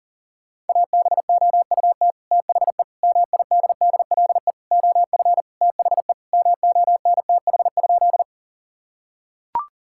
Morse code, heard as a keyed tone, about 40 wpm, 700 Hz. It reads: ABOUT THE MIDDLE OF THE MONTH?
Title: Sentences from Top 1000 Words - Review Part 6 - 40wpm